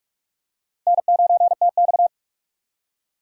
Morse code, heard as a keyed tone, about 46 wpm, 700 Hz. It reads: N9TX
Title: USA Call Signs - 4 Characters - 6x Character Spacing - 45wpm